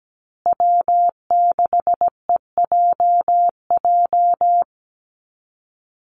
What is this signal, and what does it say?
Morse code, heard as a keyed tone, about 17 wpm, 700 Hz. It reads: W6EJJ
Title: USA Call Signs Repeated 3x 17wpm